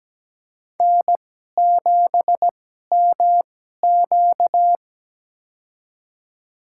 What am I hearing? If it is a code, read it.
Morse code, heard as a keyed tone, about 17 wpm, spaced as for 12 wpm, 700 Hz. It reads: N7MQ
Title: USA Call Signs - 2x Character Spacing - 17wpm